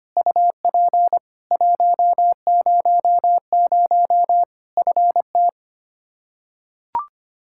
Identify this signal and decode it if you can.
Morse code, heard as a keyed tone, about 25 wpm, 700 Hz. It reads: UP 100 FT
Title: Practice QSO - Exchange 2 - Info - 25wpm